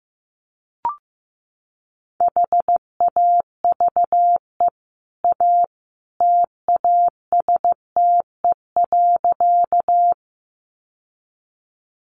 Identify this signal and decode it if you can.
Morse code, heard as a keyed tone, about 15 wpm, 700 Hz. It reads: HAVE A TASTE.
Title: Sentences from Top 1100 Words 15wpm